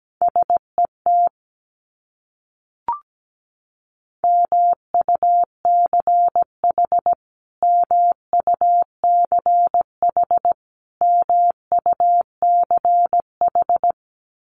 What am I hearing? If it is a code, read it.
Morse code, heard as a keyed tone, about 17 wpm, 700 Hz. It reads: SET MUCH MUCH MUCH
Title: Top 200 Words Repeated 17wpm